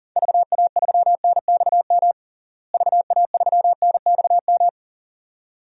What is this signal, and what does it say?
Morse code, heard as a keyed tone, about 40 wpm, 700 Hz. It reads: VA3NXM VA3NXM
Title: DXCC-100 Call Signs - 6 Characters - Repeated 2x - 40wpm